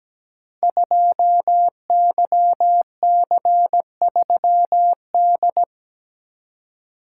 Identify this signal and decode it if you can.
Morse code, heard as a keyed tone, about 17 wpm, 700 Hz. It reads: 2YC3D
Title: Code Groups - 5 Letter:Numbers - 17wpm